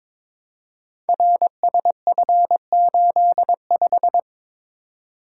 Morse code, heard as a keyed tone, about 22 wpm, 700 Hz. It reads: RSF85
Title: Code Groups - 5 Letter:Numbers - 22wpm